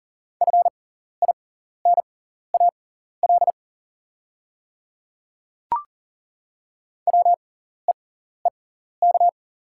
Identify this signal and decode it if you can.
Morse code, heard as a keyed tone, about 40 wpm, spaced as for 14 wpm, 690 Hz. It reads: FINAL WEEK
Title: 6x Character Spacing - Top 500 Words - 40wpm